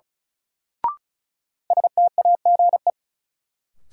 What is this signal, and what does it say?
Morse code, heard as a keyed tone, about 34 wpm, 700 Hz. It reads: STAGE
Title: Sentences from Top 1200 Words - New Word Drill - 35wpm